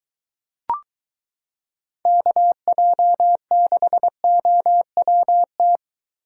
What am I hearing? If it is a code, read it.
Morse code, heard as a keyed tone, about 23 wpm, 700 Hz. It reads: KJ6OWT